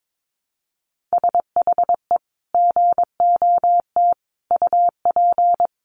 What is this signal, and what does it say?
Morse code, heard as a keyed tone, about 22 wpm, 700 Hz. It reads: SHE GOT UP
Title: Sentences from Top 1100 Words 22wpm